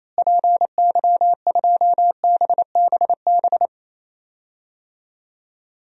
Morse code, heard as a keyed tone, about 28 wpm, 700 Hz. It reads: PY2BBB
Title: DXCC-100 Call Signs - 6 Characters - Repeated 2x - 28wpm